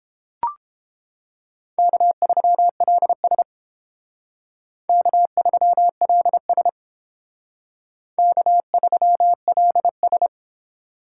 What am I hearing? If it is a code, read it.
Morse code, heard as a keyed tone, about 31 wpm, 700 Hz. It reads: K3LS K3LS K3LS